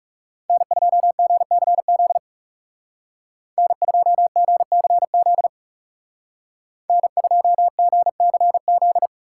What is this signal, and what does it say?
Morse code, heard as a keyed tone, about 38 wpm, 700 Hz. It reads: N2GCZ N2GCZ N2GCZ